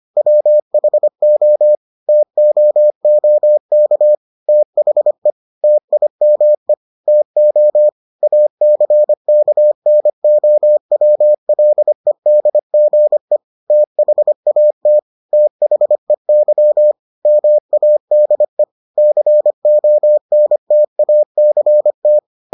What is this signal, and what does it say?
Morse code, heard as a keyed tone, about 25 wpm, 600 Hz. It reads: WHO TOOK THE TIME TO ACKNOWLEDGE THAT THEY MADE CONTACT